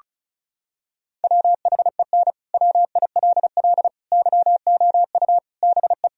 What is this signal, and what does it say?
Morse code, heard as a keyed tone, about 36 wpm, 700 Hz. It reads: WHEN WILL YOU BE
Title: Sentences from Top 1000 Words - Review Part 4 - 35wpm